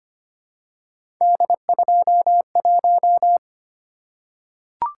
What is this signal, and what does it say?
Morse code, heard as a keyed tone, about 25 wpm, 700 Hz. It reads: D21
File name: Three Letter-Number Combinations - 25wpm